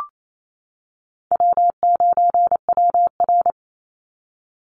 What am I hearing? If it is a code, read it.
Morse code, heard as a keyed tone, about 28 wpm, 700 Hz. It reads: W9WR
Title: Easy Call Signs - 28wpm